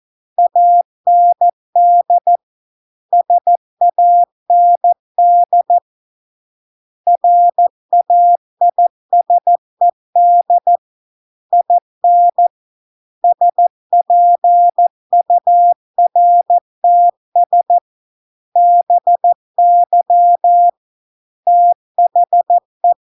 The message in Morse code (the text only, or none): AND SAND RAISED IN SPURTS BY THE